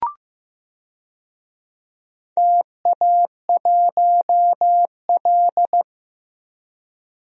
Morse code, heard as a keyed tone, about 16 wpm, 700 Hz. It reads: TA1L
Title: DXCC-100 Call Signs - 4 Characters - Repeated 2x - 15wpm